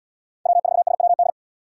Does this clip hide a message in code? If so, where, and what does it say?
Yes: Morse code from 0.5 to 1.3 s, heard as a keyed tone, about 70 wpm, 700 Hz. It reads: W4SWD